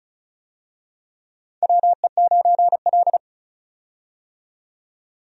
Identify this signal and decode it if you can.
Morse code, heard as a keyed tone, about 34 wpm, 700 Hz. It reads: WE9L
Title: USA Call Signs - 4 Characters - 35wpm